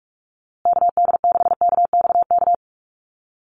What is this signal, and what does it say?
Morse code, heard as a keyed tone, about 44 wpm, 700 Hz. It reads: KD6KXK
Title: USA Call Signs Repeated 2x 45wpm